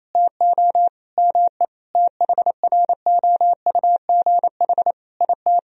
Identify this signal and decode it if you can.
Morse code, heard as a keyed tone, about 27 wpm, 700 Hz. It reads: TO ME THROUGH IT
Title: Sentences from Top 1000 Words - Review Part 1 - 28wpm